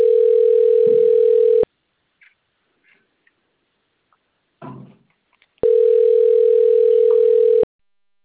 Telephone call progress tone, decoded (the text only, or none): ringback tone